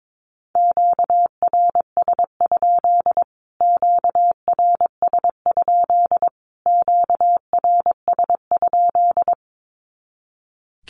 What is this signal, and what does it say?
Morse code, heard as a keyed tone, about 22 wpm, 700 Hz. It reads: QRS? QRS? QRS?
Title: Q-Codes Repeated 3x 22wpm